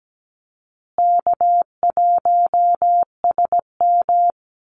Morse code, heard as a keyed tone, about 17 wpm, 700 Hz. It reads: K1SM
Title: Easy Call Signs Repeated 2x - 17wpm